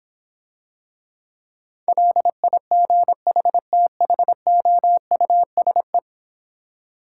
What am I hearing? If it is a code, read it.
Morse code, heard as a keyed tone, about 26 wpm, 700 Hz. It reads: LIGHTHOUSE